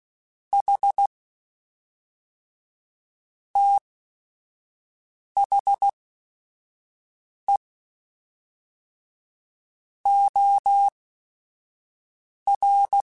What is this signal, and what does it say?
Morse code, heard as a keyed tone, about 16 wpm, spaced as for 5 wpm, 800 Hz. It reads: H THE OR
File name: Technical Subjects / Technical Subjects-mp3s / OR Gate @5